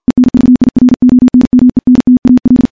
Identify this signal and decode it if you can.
Morse code, heard as a keyed tone, about 48 wpm, 250 Hz, with radio noise. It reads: ENVIRONMENTAL
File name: morse / morse-training-50WPM